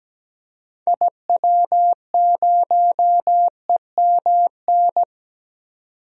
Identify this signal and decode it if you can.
Morse code, heard as a keyed tone, about 17 wpm, 700 Hz. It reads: IW0EMN
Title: DXCC-100 Call Signs - 6 Characters - 17wpm